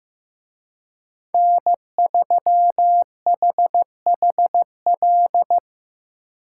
Morse code, heard as a keyed tone, about 18 wpm, 700 Hz. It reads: N3HHL
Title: Speed Racing - USA 1x3 Call Signs 15wpm